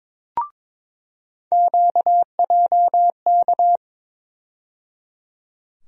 Morse code, heard as a keyed tone, about 22 wpm, 700 Hz. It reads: QJK
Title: USA Call Sign Postfixes - 3 Letters 22wpm